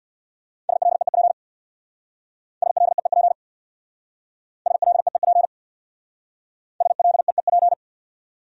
Morse code, heard as a keyed tone, about 61 wpm, 700 Hz. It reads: SLEEP SLEEP SLEEP SLEEP
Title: Speed-Racing Top 500 Words 50wpm